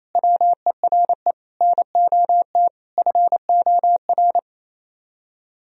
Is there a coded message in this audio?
Yes: Morse code from 0.1 to 4.4 s, heard as a keyed tone, about 37 wpm, 700 Hz. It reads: WERE NOT FOR